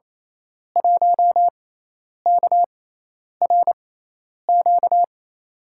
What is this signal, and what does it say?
Morse code, heard as a keyed tone, about 28 wpm, spaced as for 10 wpm, 700 Hz. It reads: 1KRQ
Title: USA Call Signs - 5 Characters - 6x Character Spacing - 28wpm